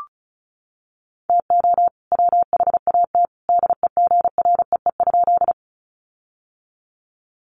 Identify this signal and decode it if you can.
Morse code, heard as a keyed tone, about 34 wpm, 700 Hz. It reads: TO WHAT DEGREE?
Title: Sentences from Top 1500 Words 35wpm